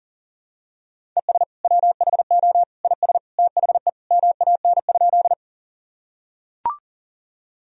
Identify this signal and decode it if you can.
Morse code, heard as a keyed tone, about 37 wpm, 700 Hz. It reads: ES WHO IS THE MAN?
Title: Sentences from Top 200 Words and Abbreviations 40wpm